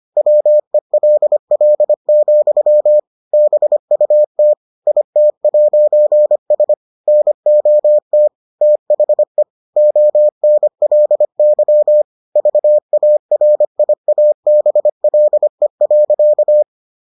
Morse code, heard as a keyed tone, about 25 wpm, 600 Hz. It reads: WELL, BUT IT'S NOT THE ONLY VARIABLE.